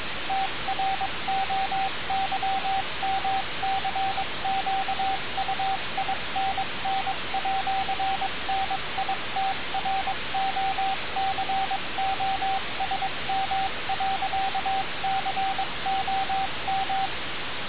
Morse code, heard as a keyed tone, about 22 wpm, 770 Hz, with radio noise. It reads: TROYMCQUINN@NITROCOSM.COM
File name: radiocap / REC z